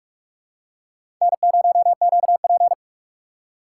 Morse code, heard as a keyed tone, about 42 wpm, 700 Hz. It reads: N0QP
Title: USA Call Signs - 3x Character Spacing - 45wpm